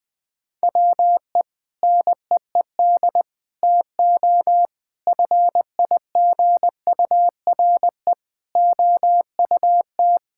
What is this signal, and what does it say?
Morse code, heard as a keyed tone, about 20 wpm, 700 Hz. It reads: WE NEED TO FIGURE OUT
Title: Sentences with 5-Word Phrases - 20wpm